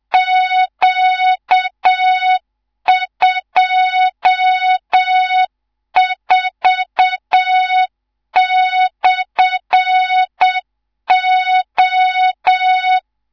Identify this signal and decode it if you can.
Morse code, heard as a keyed tone, about 7 wpm, 3700 Hz, with noise. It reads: Q24/O
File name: Morze-HA2MN / hang / 19 GYAKORLAT